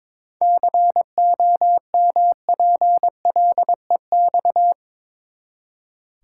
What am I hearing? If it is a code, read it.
Morse code, heard as a keyed tone, about 22 wpm, 700 Hz. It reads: COMPLEX
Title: Seven Letter Words - 22wpm